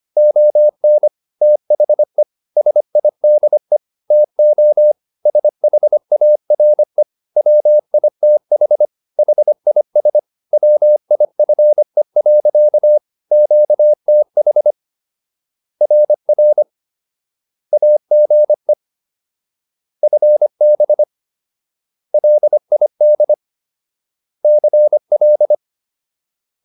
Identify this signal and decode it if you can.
Morse code, heard as a keyed tone, about 25 wpm, 600 Hz. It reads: ON THE SIDE TO SHARE WITH HIS WIFE. QTH RR AGE FB LID CL